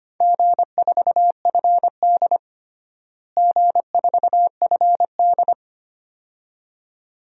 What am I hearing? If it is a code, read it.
Morse code, heard as a keyed tone, about 25 wpm, 700 Hz. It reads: G4FD G4FD
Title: DXCC-100 Call Signs - 4 Characters - Repeated 2x - 25wpm